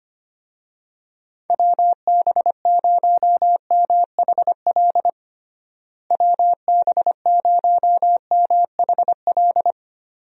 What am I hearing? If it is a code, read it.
Morse code, heard as a keyed tone, about 25 wpm, 700 Hz. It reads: WB0MHL WB0MHL
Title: USA Call Signs - 6 Characters - Repeated 2x - 25wpm